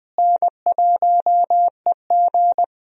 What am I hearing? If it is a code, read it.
Morse code, heard as a keyed tone, about 21 wpm, 700 Hz. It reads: N1EG
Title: USA Call Signs - 4 Characters - Repeated 3x - 20wpm